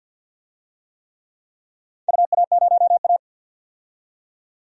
Morse code, heard as a keyed tone, about 46 wpm, 700 Hz. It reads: UA0A
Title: DXCC-100 Call Signs - 4 Characters - 50wpm